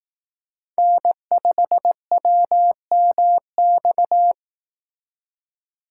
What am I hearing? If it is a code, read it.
Morse code, heard as a keyed tone, about 18 wpm, 700 Hz. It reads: N5WMX
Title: Speed Racing - USA 1x3 Call Signs 15wpm